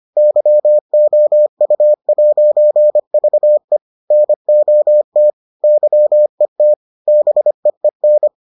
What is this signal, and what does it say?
Morse code, heard as a keyed tone, about 25 wpm, 600 Hz. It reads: YOU'VE NOT YET BEEN